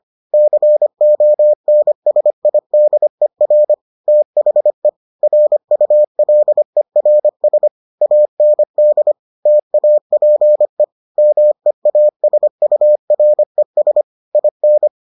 Morse code, heard as a keyed tone, about 25 wpm, 600 Hz. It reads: CONSIDER THE RULERS AND TAPE MEASURES IN